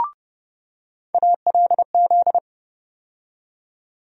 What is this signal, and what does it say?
Morse code, heard as a keyed tone, about 30 wpm, 700 Hz. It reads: ALZ
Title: USA Call Sign Postfixes - 3 Letters 30wpm